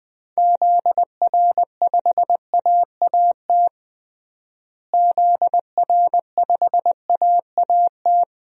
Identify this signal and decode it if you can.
Morse code, heard as a keyed tone, about 21 wpm, 700 Hz. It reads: ZR5AAT ZR5AAT
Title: DXCC-100 Call Signs - 6 Characters - Repeated 2x - 20wpm